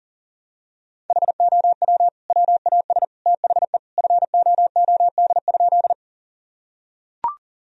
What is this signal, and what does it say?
Morse code, heard as a keyed tone, about 40 wpm, 700 Hz. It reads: HOW WAS THE FOOD?
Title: Sentences from Top 1000 Words - Review Part 1 - 40wpm